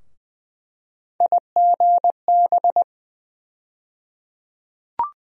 Morse code, heard as a keyed tone, about 20 wpm, 700 Hz. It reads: IGB
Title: USA Call Sign Postfixes - 3 Letters 20wpm